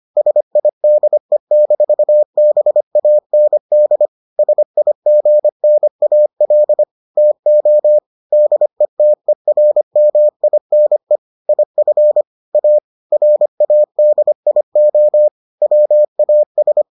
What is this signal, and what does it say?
Morse code, heard as a keyed tone, about 25 wpm, 600 Hz. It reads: SIDE-BAND SIGNAL TO DETERMINE IF A RADIO WAS